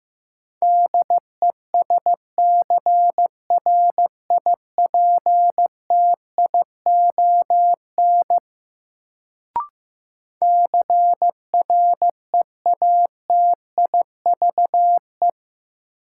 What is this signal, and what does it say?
Morse code, heard as a keyed tone, about 15 wpm, 700 Hz. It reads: DESCRIPTION CREATIVE
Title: Sentences from Top 1300 Words - New Word Drill Alt - 15wpm